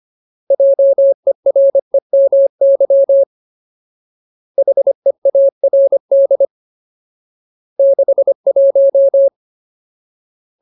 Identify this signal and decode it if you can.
Morse code, heard as a keyed tone, about 25 wpm, 550 Hz. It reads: JEREMY HEARD 61
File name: Ultimate Word List / 25WPM Ultimate